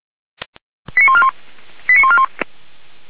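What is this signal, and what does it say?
Five-tone selective call (ZVEI), heard as quick runs of five tones, tones of about 70 ms. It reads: 89141 89141
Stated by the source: ZVEI five-tone call received on SDR.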